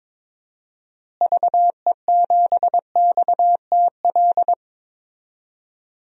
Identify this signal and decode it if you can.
Morse code, heard as a keyed tone, about 22 wpm, 700 Hz. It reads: VE7XTL
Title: DXCC-100 Call Signs - 6 Characters - Repeated 3x - 22wpm